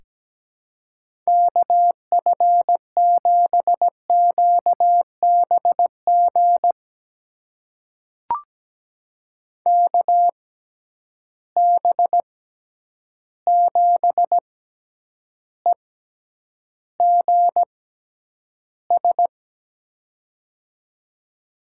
Morse code, heard as a keyed tone, about 17 wpm, spaced as for 9 wpm, 700 Hz. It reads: KF7QBG KB7EGS
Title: USA Call Signs - 6 Characters - 6x Character Spacing - 17wpm